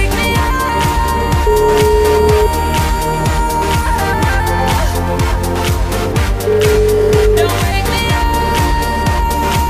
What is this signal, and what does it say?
Ringback tone (Europe, 425 Hz), with noise.